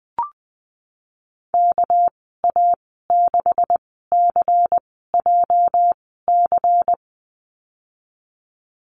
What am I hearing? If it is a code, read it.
Morse code, heard as a keyed tone, about 20 wpm, spaced as for 15 wpm, 700 Hz. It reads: KA6CJC
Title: 231 Lesson 37g - Letter x - Call Signs 20wpm